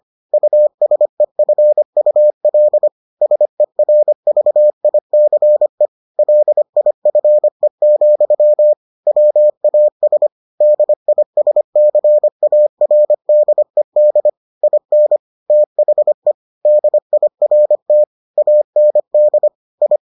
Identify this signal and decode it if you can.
Morse code, heard as a keyed tone, about 25 wpm, 600 Hz. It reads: USEFUL SERVICE LIFE, WAS DISCARDED IN THE DIRT AND I